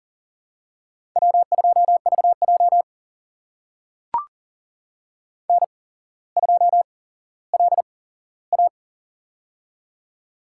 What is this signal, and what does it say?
Morse code, heard as a keyed tone, about 40 wpm, spaced as for 17 wpm, 700 Hz. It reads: W2VJ N2LA
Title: USA Call Signs - 4 Characters - 8x Character Spacing - 40wpm